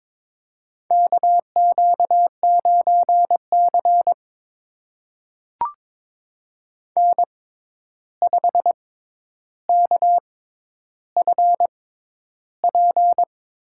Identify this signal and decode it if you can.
Morse code, heard as a keyed tone, about 22 wpm, spaced as for 11 wpm, 700 Hz. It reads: KQ9C N5KFP
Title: USA Call Signs - 6x Character Spacing - 22wpm